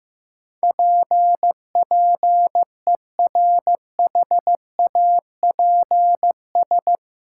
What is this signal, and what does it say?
Morse code, heard as a keyed tone, about 15 wpm, 700 Hz. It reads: PPERHAPS
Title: Sentences from Top 1100 Words - New Word Drill Alt - 15wpm